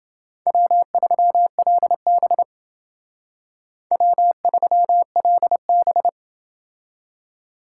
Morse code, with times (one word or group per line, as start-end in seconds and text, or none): 0.5-2.4 W3LB
3.9-6.1 W3LB